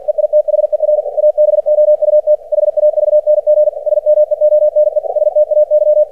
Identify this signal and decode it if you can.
Morse code, heard as a keyed tone, about 50 wpm, 600 Hz. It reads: EITHER ABOUT HAVNG SMEONE STAY
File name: archived / CRSnet